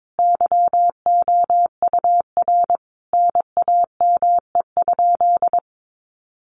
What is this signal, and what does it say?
Morse code, heard as a keyed tone, about 22 wpm, 700 Hz. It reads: YOUR NAME?
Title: Sentences from Top 200 Words 22wpm